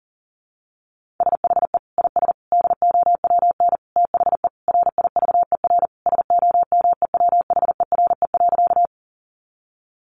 Morse code, heard as a keyed tone, about 40 wpm, 700 Hz. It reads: SHE IS DOWN THE RIVER SOMEWHERE.
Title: Sentences from Top 1400 Words 40wpm